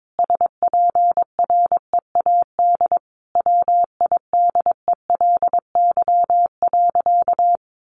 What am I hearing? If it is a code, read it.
Morse code, heard as a keyed tone, about 22 wpm, 700 Hz. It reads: SPREAD WIDELY.
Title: Sentences from Top 1200 Words 22wpm